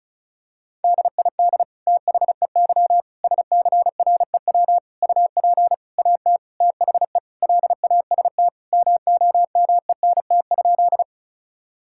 Morse code, heard as a keyed tone, about 36 wpm, 700 Hz. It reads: DID THEY SCREW UP AT THE LAST MOMENT?
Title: Sentences from Top 1600 Words 35wpm